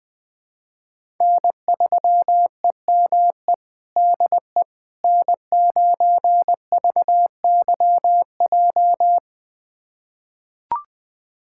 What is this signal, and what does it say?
Morse code, heard as a keyed tone, about 21 wpm, 700 Hz. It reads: N3EME DE N9VYJ
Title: USA Callsign DE Callsign - 5 Characters - 20wpm